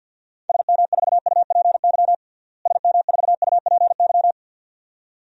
Morse code, heard as a keyed tone, about 48 wpm, 700 Hz. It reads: SM4UPY SM4UPY
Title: DXCC-100 Call Signs - 6 Characters - Repeated 2x - 50wpm